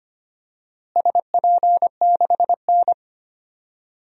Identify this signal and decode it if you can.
Morse code, heard as a keyed tone, about 25 wpm, 700 Hz. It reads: SP6N
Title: DXCC-100 Call Signs - 4 Characters - Repeated 3x - 25wpm